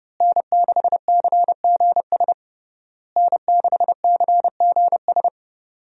Morse code, heard as a keyed tone, about 31 wpm, 700 Hz. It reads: N6CGS N6CGS
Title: USA Call Signs - 5 Characters - Repeated 3x - 30wpm